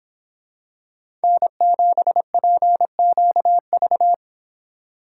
Morse code, heard as a keyed tone, about 26 wpm, 700 Hz. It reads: N7PQV